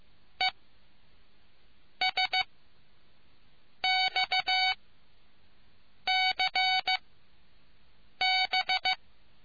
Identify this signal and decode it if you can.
Morse code, heard as a keyed tone, about 15 wpm, spaced as for 5 wpm, 750 Hz. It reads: ESXCB